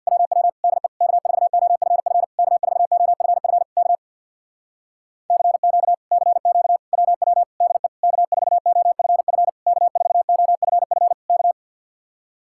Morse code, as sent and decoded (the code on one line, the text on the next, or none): .-- .-- / -.. / -.- ....- --- ..-. ..-. / -.- ....- --- ..-. ..-. / -.- / -.-. --.- / -.-. --.- / .-- .-- / -.. / -.- ....- --- ..-. ..-. / -.- ....- --- ..-. ..-. / -.-
WW DE K4OFF K4OFF K CQ CQ WW DE K4OFF K4OFF K